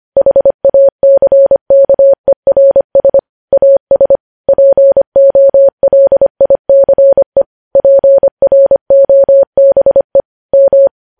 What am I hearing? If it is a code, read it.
Morse code, heard as a keyed tone, about 25 wpm, 550 Hz. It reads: HACKERS AS POLICE PROBE M
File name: podcast / News headlines-1x-25x25